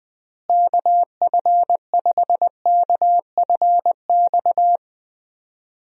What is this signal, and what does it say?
Morse code, heard as a keyed tone, about 21 wpm, 700 Hz. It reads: KF5KFX
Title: USA Call Signs - 6 Characters - Repeated 3x - 20wpm